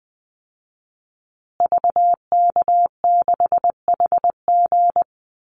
Morse code, heard as a keyed tone, about 20 wpm, 700 Hz. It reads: VK6HG
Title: Call Signs Repeated 2x - 20wpm